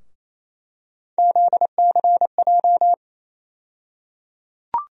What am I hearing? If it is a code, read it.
Morse code, heard as a keyed tone, about 28 wpm, 700 Hz. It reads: ZCJ
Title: USA Call Sign Postfixes - 3 Letters 28wpm